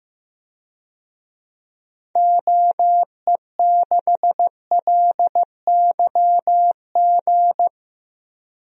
Morse code, heard as a keyed tone, about 15 wpm, 700 Hz. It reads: OE6LYG
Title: DXCC-100 Call Signs - 6 Characters - Repeated 2x - 15wpm